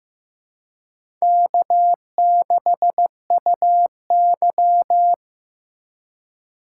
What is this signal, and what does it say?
Morse code, heard as a keyed tone, about 15 wpm, 700 Hz. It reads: K6UY
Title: USA Call Signs - 4 Characters - 2x Character Spacing - 15wpm